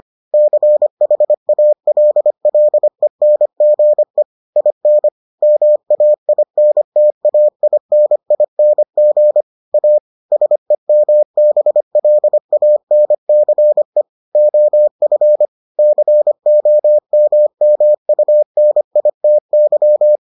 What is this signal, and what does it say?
Morse code, heard as a keyed tone, about 25 wpm, 600 Hz. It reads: CHALLENGE IN MAINTAINING A SEMBLANCE OF COMMUNITY